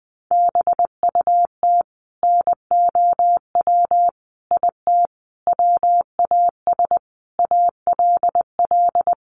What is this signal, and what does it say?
Morse code, heard as a keyed tone, about 20 wpm, 700 Hz. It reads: BUT NOW IT WAS ALL
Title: Sentences from Top 700 Words 20wpm